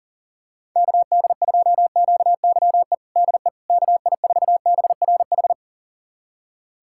Morse code, heard as a keyed tone, about 40 wpm, 700 Hz. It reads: KD2QYE DE KI4BRH
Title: USA Callsign DE Callsign 40wpm